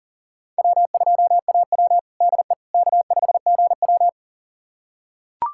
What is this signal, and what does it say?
Morse code, heard as a keyed tone, about 40 wpm, 700 Hz. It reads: W2AW DE K5GW
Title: USA Callsign DE Callsign - 4 Characters - 40wpm